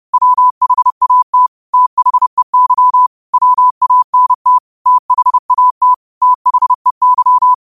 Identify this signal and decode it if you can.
Morse code, heard as a keyed tone, about 30 wpm, 1000 Hz. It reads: WHAT THEY WANT THAT THEY